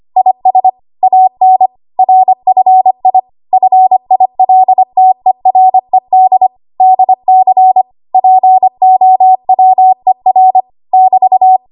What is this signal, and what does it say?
Morse code, heard as a keyed tone, about 25 wpm, 750 Hz. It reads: IS AN RFI FILTERED DC POWER <BT>
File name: Morse / 25WPM